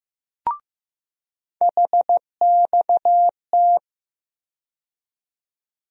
Morse code, heard as a keyed tone, about 15 wpm, 700 Hz. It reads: HXT
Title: USA Call Sign Postfixes - 3 Letters 15wpm